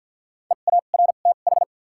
Morse code, heard as a keyed tone, about 50 wpm, spaced as for 36 wpm, 700 Hz. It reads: EARTH
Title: 2x Character Spacing - Top 200 Words - 50wpm